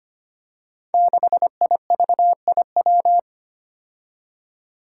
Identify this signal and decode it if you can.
Morse code, heard as a keyed tone, about 25 wpm, 700 Hz. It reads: 6IVIW
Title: Five Challenging Letter-Number Combinations 25wpm